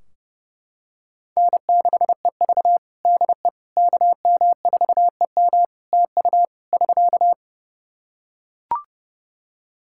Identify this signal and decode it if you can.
Morse code, heard as a keyed tone, about 28 wpm, 700 Hz. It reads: N6EV DE KM4EM TU <SK>
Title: Practice QSO - Exchange 4 - Ending - 30wpm